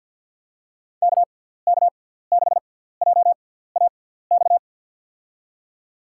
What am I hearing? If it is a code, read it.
Morse code, heard as a keyed tone, about 51 wpm, spaced as for 17 wpm, 700 Hz. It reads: KK6JAX
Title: USA Call Signs - 6 Characters - 6x Character Spacing - 50wpm